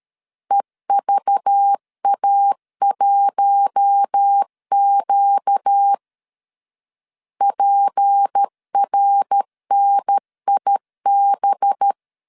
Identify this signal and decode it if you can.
Morse code, heard as a keyed tone, about 13 wpm, 800 Hz. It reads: EVA1Q PRNIB
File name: LEZIONI / 6 PKX0 / 6 P 12wpm 2033